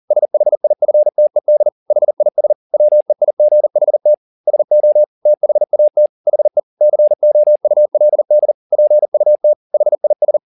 Morse code, heard as a keyed tone, about 40 wpm, 600 Hz. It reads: SHIFTED HIS WEIGHT SO THAT HE COULD PUT HIS
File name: the old man and the sea 2 40wpm